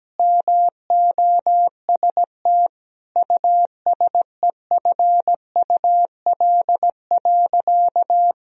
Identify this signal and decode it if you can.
Morse code, heard as a keyed tone, about 17 wpm, 700 Hz. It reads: MOST USEFUL.
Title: Sentences from Top 1000 Words - Review Part 4 - 17wpm